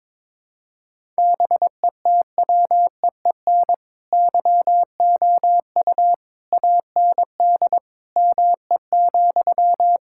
Morse code, heard as a keyed tone, about 22 wpm, 700 Hz. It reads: BETWEEN YOU AND ME,
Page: morse-code-ninja-practice-22wpm